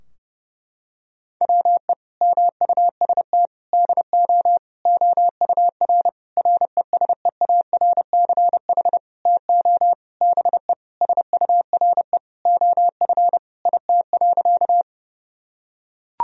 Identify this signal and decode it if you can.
Morse code, heard as a keyed tone, about 30 wpm, 700 Hz. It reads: WE MUST DO OUR RESEARCH TO BE SURE OF IT.
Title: Sentences from Top 1000 Words - Review Part 4 - 30wpm